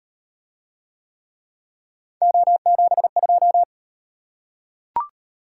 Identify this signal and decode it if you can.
Morse code, heard as a keyed tone, about 34 wpm, 700 Hz. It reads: O72